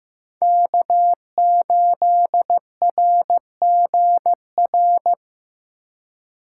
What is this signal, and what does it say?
Morse code, heard as a keyed tone, about 18 wpm, 700 Hz. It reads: K8RGR
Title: Speed Racing - USA Call Signs 15wpm